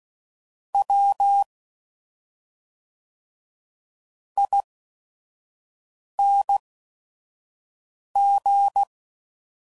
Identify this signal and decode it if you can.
Morse code, heard as a keyed tone, about 16 wpm, spaced as for 5 wpm, 800 Hz. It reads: W ING
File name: Suffixes / Special ING's / SlowToFaster ING's